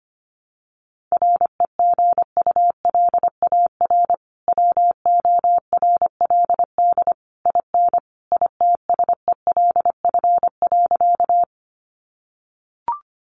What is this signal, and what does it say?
Morse code, heard as a keyed tone, about 25 wpm, 700 Hz. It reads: REGULAR WORLD IN ITSELF.
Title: Sentences from Top 900 Words 25wpm